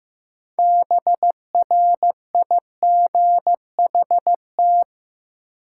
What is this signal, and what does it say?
Morse code, heard as a keyed tone, about 15 wpm, 700 Hz. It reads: BRIGHT
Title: Sentences from Top 1000 Words - New Word Drill - 15wpm